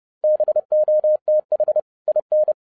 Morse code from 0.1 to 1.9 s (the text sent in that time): BOTH